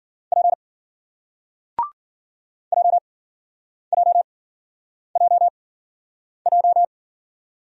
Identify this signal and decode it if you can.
Morse code, heard as a keyed tone, about 52 wpm, spaced as for 9 wpm, 700 Hz. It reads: P JJJJ